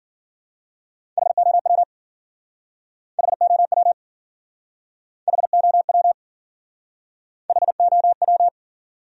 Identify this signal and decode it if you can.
Morse code, heard as a keyed tone, about 53 wpm, 700 Hz. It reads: HOW HOW HOW HOW